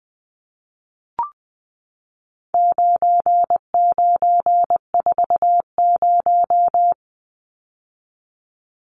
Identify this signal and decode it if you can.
Morse code, heard as a keyed tone, about 20 wpm, 700 Hz. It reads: 9940
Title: Numbers - 4-Digit 20wpm